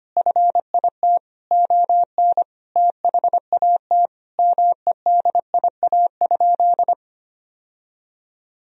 Morse code, heard as a keyed tone, about 25 wpm, 700 Hz. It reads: FIT ON THAT MEDIA?
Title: Sentences from Top 1000 Words - Review Part 4 - 25wpm